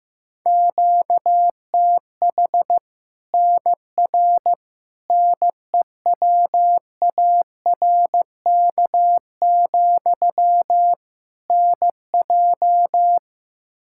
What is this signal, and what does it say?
Morse code, heard as a keyed tone, about 15 wpm, 700 Hz. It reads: QTH NR NEWARK, NJ
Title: Practice QSO - Exchange 1 - Beginning - 15wpm